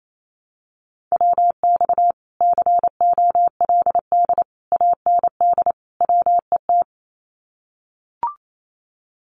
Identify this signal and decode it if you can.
Morse code, heard as a keyed tone, about 28 wpm, 700 Hz. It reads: WX COLD AND WET
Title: Common QSO Phrases Encore 28wpm